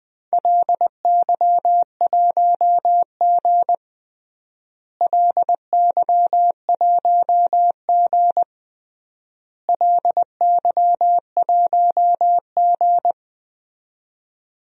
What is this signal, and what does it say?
Morse code, heard as a keyed tone, about 21 wpm, 700 Hz. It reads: LY1G LY1G LY1G
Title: DXCC-100 Call Signs - 4 Characters - Repeated 3x - 20wpm